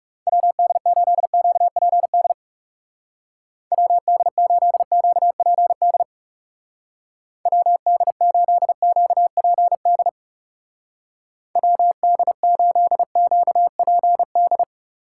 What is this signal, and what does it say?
Morse code, heard as a keyed tone, about 37 wpm, 700 Hz. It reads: WD8QPD WD8QPD WD8QPD WD8QPD
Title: Speed Racing - USA 2x3 Call Signs 30wpm